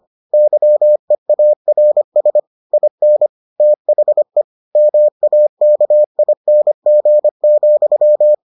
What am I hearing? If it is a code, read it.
Morse code, heard as a keyed tone, about 25 wpm, 600 Hz. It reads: YEARS IN THE MAKING,